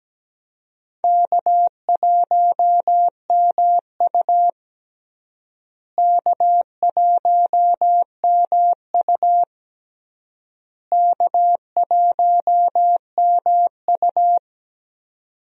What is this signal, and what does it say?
Morse code, heard as a keyed tone, about 17 wpm, 700 Hz. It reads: K1MU K1MU K1MU
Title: USA Call Signs - 4 Characters - Repeated 3x - 17wpm